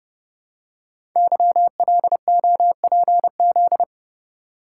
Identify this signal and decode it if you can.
Morse code, heard as a keyed tone, about 30 wpm, 700 Hz. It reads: YLOPZ
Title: Code Groups - 5 Letters - 30wpm